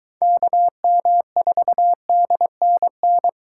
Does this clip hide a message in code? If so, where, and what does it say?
Yes: Morse code from 0.2 to 3.3 s, heard as a keyed tone, about 23 wpm, 700 Hz. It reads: KM4DNN